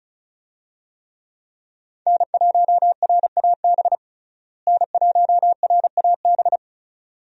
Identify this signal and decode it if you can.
Morse code, heard as a keyed tone, about 34 wpm, 700 Hz. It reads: N1RAB N1RAB
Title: USA Call Signs - 5 Characters - Repeated 2x - 35wpm